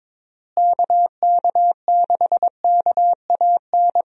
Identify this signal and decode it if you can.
Morse code, heard as a keyed tone, about 22 wpm, 700 Hz. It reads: KK6KAN
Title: USA Call Signs - 6 Characters - Repeated 2x - 22wpm